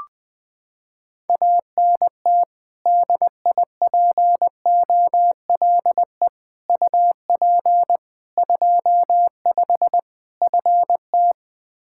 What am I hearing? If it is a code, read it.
Morse code, heard as a keyed tone, about 21 wpm, 700 Hz. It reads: ANT DIPOLE UP 25 FT
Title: Practice QSO - Exchange 2 - Info - 20wpm